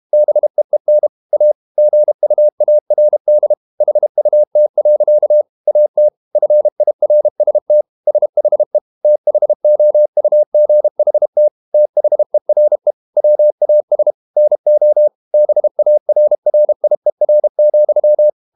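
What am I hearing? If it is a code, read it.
Morse code, heard as a keyed tone, about 32 wpm, 600 Hz. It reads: BEEN A GUARD HUT. AT FIRST SHE THOUGHT THERE WAS NO BARRIER,
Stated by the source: Morse code practice at 32 wpm from CW med Gnister, reading Ken Follett's novel Never in English.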